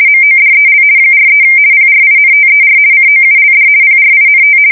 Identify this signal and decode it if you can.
RTTY, about 50 baud. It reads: MODEL 43 PRINTER COULD PRINT
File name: fsk170-50B / teletype-corp